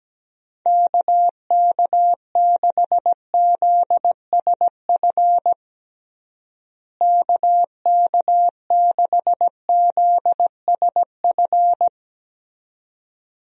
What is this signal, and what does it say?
Morse code, heard as a keyed tone, about 17 wpm, 700 Hz. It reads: KK6ZSF KK6ZSF
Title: USA Call Signs - 6 Characters - Repeated 2x - 17wpm